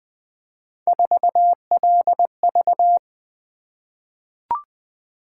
Morse code, heard as a keyed tone, about 20 wpm, 700 Hz. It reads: 4LV
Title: Three Letter-Number Combinations - 20wpm